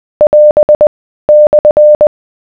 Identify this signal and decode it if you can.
Morse code, heard as a keyed tone, about 20 wpm, 600 Hz. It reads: <AS> /